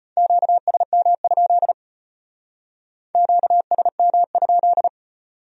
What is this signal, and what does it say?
Morse code, heard as a keyed tone, about 33 wpm, 700 Hz. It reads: QSM? QSM?